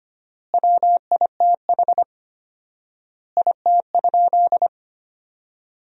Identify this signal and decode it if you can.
Morse code, heard as a keyed tone, about 25 wpm, 700 Hz. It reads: WITH IT?
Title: 4x Word-Spacing - Sentences from Top 100 Words - 25wpm